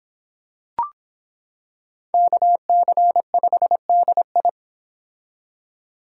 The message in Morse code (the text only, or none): KC5DI